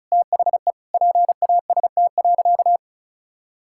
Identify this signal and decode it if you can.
Morse code, heard as a keyed tone, about 36 wpm, 700 Hz. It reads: THE PAST.
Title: Sentences from Top 1000 Words - Review Part 6 - 35wpm